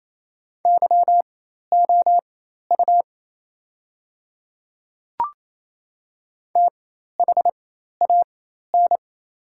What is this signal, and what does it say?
Morse code, heard as a keyed tone, about 28 wpm, spaced as for 13 wpm, 700 Hz. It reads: YOU THAN
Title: 4x Character Spacing - Top 100 Words - 28wpm